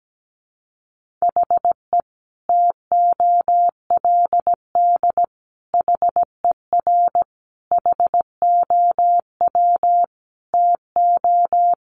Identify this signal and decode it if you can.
Morse code, heard as a keyed tone, about 17 wpm, 700 Hz. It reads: HE TOLD HER HOW TO
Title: Sentences from Top 1500 Words 17wpm